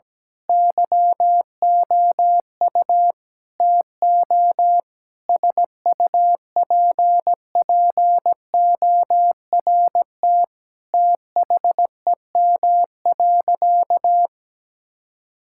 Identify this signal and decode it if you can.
Morse code, heard as a keyed tone, about 17 wpm, 700 Hz. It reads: YOU TO SUPPORT THEM.
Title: Sentences from Top 1000 Words - Review Part 5 - 17wpm